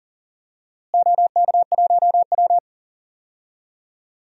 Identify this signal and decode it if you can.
Morse code, heard as a keyed tone, about 40 wpm, 700 Hz. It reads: OK1W
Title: DXCC-100 Call Signs - 4 Characters - Repeated 2x - 40wpm